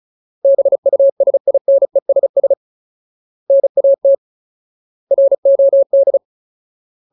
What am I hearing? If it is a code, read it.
Morse code, heard as a keyed tone, about 34 wpm, 550 Hz. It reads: BUSINESS NAT ROD
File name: Ultimate Word List / 35WPM Ultimate